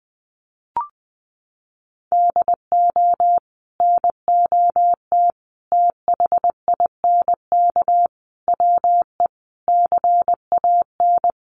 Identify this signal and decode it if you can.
Morse code, heard as a keyed tone, about 19 wpm, 700 Hz. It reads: DO NOT THINK WE CAN
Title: Sets of 5 Words Encore2 20wpm